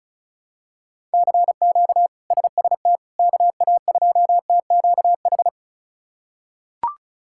Morse code, heard as a keyed tone, about 36 wpm, 700 Hz. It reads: CQ SST KA2TQH
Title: K1USN SST QSO Practice - 35wpm